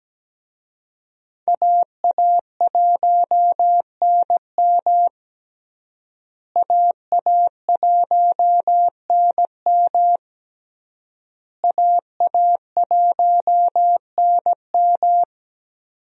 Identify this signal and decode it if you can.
Morse code, heard as a keyed tone, about 17 wpm, 700 Hz. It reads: AA1NM AA1NM AA1NM
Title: USA Call Signs - 5 Characters - Repeated 3x - 17wpm